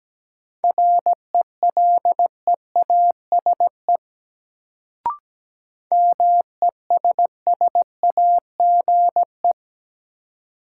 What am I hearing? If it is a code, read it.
Morse code, heard as a keyed tone, about 17 wpm, 700 Hz. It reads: RELEASE MESSAGE
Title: Sentences from Top 1100 Words - New Word Drill - 17wpm